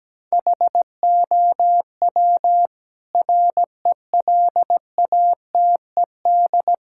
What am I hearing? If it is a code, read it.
Morse code, heard as a keyed tone, about 17 wpm, 700 Hz. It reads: HOW RELATED
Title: Sentences from Top 1000 Words - Review Part 5 - 17wpm